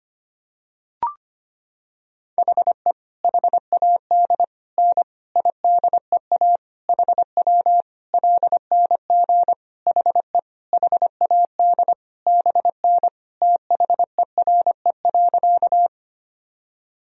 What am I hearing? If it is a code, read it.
Morse code, heard as a keyed tone, about 25 wpm, 700 Hz. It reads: HE HAD N IDEA HW LNG HE HAD BN THERE.
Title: Sentences from Top 300 Words and Abbreviations 25wpm